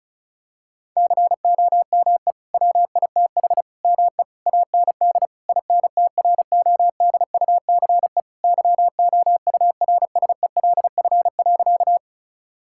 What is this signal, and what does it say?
Morse code, heard as a keyed tone, about 36 wpm, 690 Hz. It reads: COME WITH ME AND INTRODUCE YOURSELF.
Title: Sentences from Top 1600 Words 35wpm